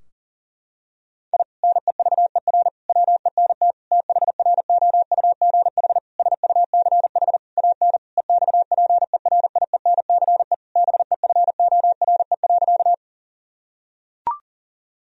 Morse code, heard as a keyed tone, about 40 wpm, 700 Hz. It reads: I NEVER WENT THROUGH SUCH AN EXPERIENCE BEFORE.
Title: Sentences from Top 1000 Words - Review Part 3 - 40wpm